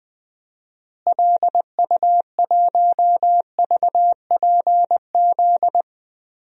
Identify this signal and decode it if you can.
Morse code, heard as a keyed tone, about 20 wpm, 700 Hz. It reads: LU1VPZ
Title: DXCC-100 Call Signs - 6 Characters - Repeated 2x - 20wpm